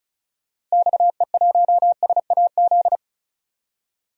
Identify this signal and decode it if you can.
Morse code, heard as a keyed tone, about 36 wpm, 700 Hz. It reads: XE1SAZ
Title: DXCC-100 Call Signs - 6 Characters - 35wpm